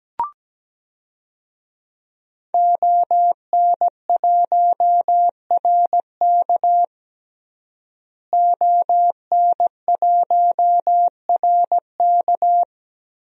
Morse code, heard as a keyed tone, about 17 wpm, 700 Hz. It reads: ON1RK ON1RK
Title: DXCC-100 Call Signs - 5 Characters - Repeated 3x - 17wpm